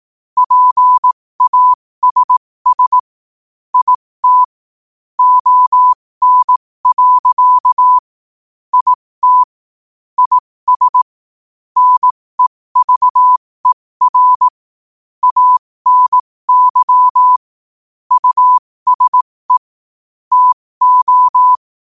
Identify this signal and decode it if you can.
Morse code, heard as a keyed tone, about 18 wpm, spaced as for 15 wpm, 1000 Hz. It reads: PASS IT ON. IT IS NEVER ANY USE TO
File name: qotdfiles / 15wpm